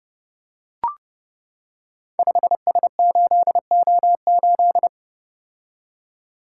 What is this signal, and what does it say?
Morse code, heard as a keyed tone, about 28 wpm, 700 Hz. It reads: 5S8O8
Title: Code Groups - 5 Letter:Numbers - 30wpm